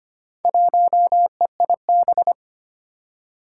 Morse code, heard as a keyed tone, about 25 wpm, 700 Hz. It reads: 1EIB
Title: Four Challenging Letter-Number Combinations 25wpm